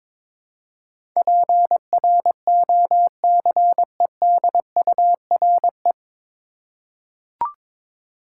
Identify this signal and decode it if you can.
Morse code, heard as a keyed tone, about 22 wpm, 700 Hz. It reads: PROCEDURE
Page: Morse Code Ninja practice 22wpm